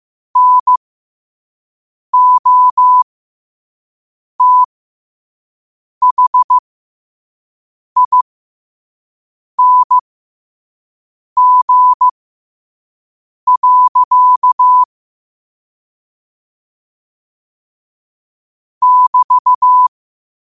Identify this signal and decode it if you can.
Morse code, heard as a keyed tone, about 15 wpm, spaced as for 5 wpm, 1000 Hz. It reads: NOTHING. <BT>